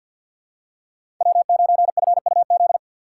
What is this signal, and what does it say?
Morse code, heard as a keyed tone, about 50 wpm, 700 Hz. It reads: W9FUZ
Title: USA Call Signs - 2x Character Spacing - 50wpm